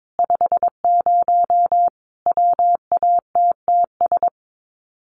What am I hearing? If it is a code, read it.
Morse code, heard as a keyed tone, about 22 wpm, 700 Hz. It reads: 50 WATTS
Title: Common QSO Phrases 22wpm